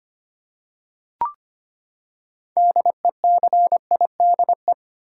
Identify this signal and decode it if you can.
Morse code, heard as a keyed tone, about 25 wpm, 700 Hz. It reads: DECIDE
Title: Six Letter Words - 25wpm